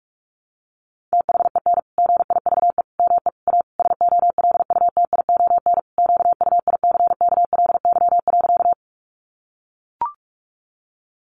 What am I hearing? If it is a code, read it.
Morse code, heard as a keyed tone, about 44 wpm, 700 Hz. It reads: THEN GIVE ME A SOLUTION QUICKLY.
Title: Sentences from Top 1300 Words 45wpm